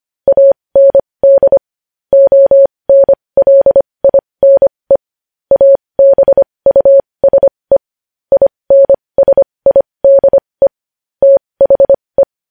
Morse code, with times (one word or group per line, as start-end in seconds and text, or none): 0.3-1.6 AND
2.1-5.0 ONLINE
5.5-7.8 ABUSE
8.3-10.7 INSIDE
11.2-12.2 THE